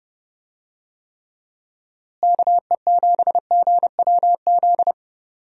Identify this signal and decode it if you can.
Morse code, heard as a keyed tone, about 30 wpm, 700 Hz. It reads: KE7GWZ
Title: USA Call Signs - 6 Characters - Repeated 3x - 30wpm